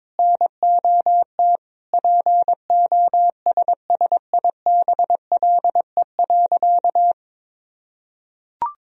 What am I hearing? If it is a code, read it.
Morse code, heard as a keyed tone, about 22 wpm, 700 Hz. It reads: NOT POSSIBLE.
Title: Sentences from Top 1000 Words - Review Part 1 - 22wpm